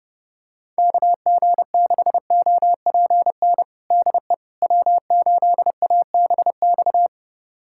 Morse code, heard as a keyed tone, about 30 wpm, 700 Hz. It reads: KG6OPN DE W8ABX
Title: USA Callsign DE Callsign 30wpm